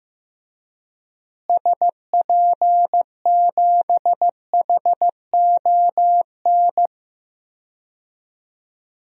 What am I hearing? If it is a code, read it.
Morse code, heard as a keyed tone, about 15 wpm, 700 Hz. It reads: SP7HON
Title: DXCC-100 Call Signs - 6 Characters - Repeated 2x - 15wpm